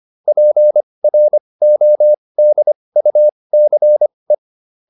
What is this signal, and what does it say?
Morse code, heard as a keyed tone, about 25 wpm, spaced as for 20 wpm, 600 Hz. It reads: PRODUCE